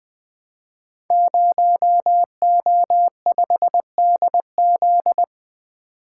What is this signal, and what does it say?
Morse code, heard as a keyed tone, about 20 wpm, 700 Hz. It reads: 0O5DZ
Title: Code Groups - 5 Letter:Numbers - 20wpm